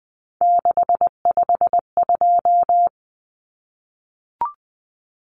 Morse code, heard as a keyed tone, about 20 wpm, 700 Hz. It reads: 652
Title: Numbers - 3-Digit 20wpm